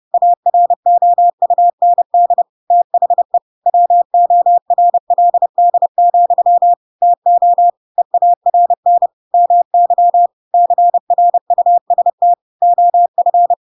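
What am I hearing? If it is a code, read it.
Morse code, heard as a keyed tone, about 30 wpm, 700 Hz. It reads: AROUND THE WORLD, TO EARN MY CRUST OF
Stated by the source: Chapter 2 of The Adventures of Pinocchio read as a book in Morse code (CW) at 30 wpm, by ik1hge.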